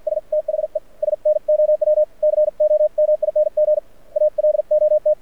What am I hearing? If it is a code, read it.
Morse code, heard as a keyed tone, about 53 wpm, spaced as for 46 wpm, 590 Hz. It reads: S THE SNOW COMING ALOT
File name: archived / CRSnet